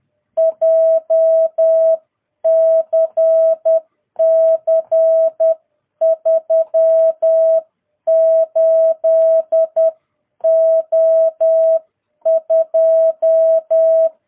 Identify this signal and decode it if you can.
Morse code, heard as a keyed tone, about 10 wpm, 660 Hz. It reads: JCC 38O2
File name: jh8cbh / 2022cw